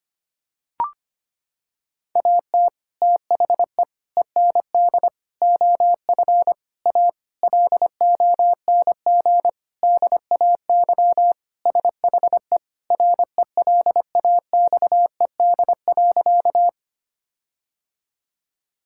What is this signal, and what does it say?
Morse code, heard as a keyed tone, about 25 wpm, 700 Hz. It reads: AT THE END OF A LONG DAY SHE RELAXED.
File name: practice / 25wpm / Sentences with 4-Word Phrases - 25wpm